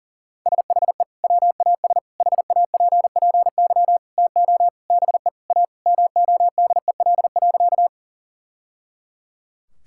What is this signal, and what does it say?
Morse code, heard as a keyed tone, about 37 wpm, 700 Hz. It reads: SHE WAS HAPPY TO BE A MODEL.
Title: Sentences from Top 1000 Words - Review Part 5 - 40wpm